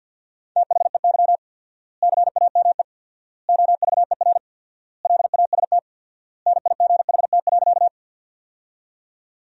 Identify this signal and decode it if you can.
Morse code, heard as a keyed tone, about 50 wpm, 700 Hz. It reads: THEY CAME OVER LAST NIGHT.
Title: 4x Word-Spacing - Sentences from Top 300 Words - 50wpm